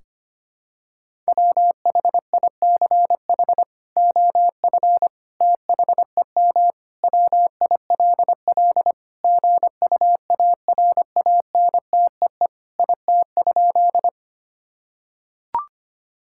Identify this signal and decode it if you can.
Morse code, heard as a keyed tone, about 25 wpm, 700 Hz. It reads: WHICH OF THEM WILL GUARANTEE IT?
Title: Sentences from Top 1600 Words 25wpm